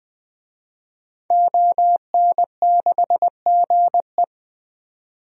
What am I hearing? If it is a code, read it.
Morse code, heard as a keyed tone, about 20 wpm, 700 Hz. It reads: ON6GE
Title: DXCC-100 Call Signs - 5 Characters - 20wpm